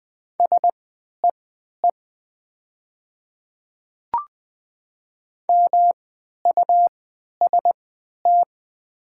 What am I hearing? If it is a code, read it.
Morse code, heard as a keyed tone, about 20 wpm, spaced as for 11 wpm, 690 Hz. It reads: SEE MUST
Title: 3x Character Spacing - Top 300 Words - 20wpm